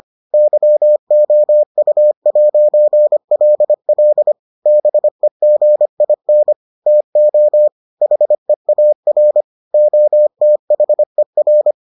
Morse code, heard as a keyed tone, about 25 wpm, 600 Hz. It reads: YOU'LL BEGIN TO HEAR OTHER